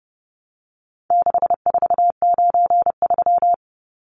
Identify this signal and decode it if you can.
Morse code, heard as a keyed tone, about 30 wpm, 700 Hz. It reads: 6493
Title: Numbers - 4-Digit 30wpm